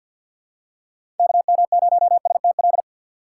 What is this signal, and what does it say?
Morse code, heard as a keyed tone, about 48 wpm, 700 Hz. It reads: KM0STL
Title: USA Call Signs - 6 Characters - 6x Character Spacing - 50wpm